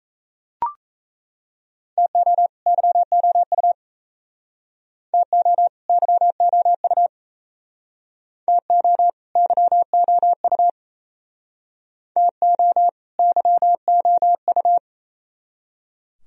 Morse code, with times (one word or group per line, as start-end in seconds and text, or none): 2.0-2.5 TO
2.7-3.7 YOU
5.1-5.7 TO
5.9-7.1 YOU
8.5-9.1 TO
9.4-10.7 YOU
12.2-12.9 TO
13.2-14.8 YOU